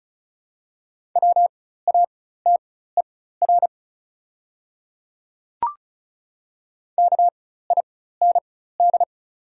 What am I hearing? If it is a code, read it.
Morse code, heard as a keyed tone, about 36 wpm, spaced as for 16 wpm, 690 Hz. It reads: WATER KIND
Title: 4x Character Spacing - Top 200 Words - 35wpm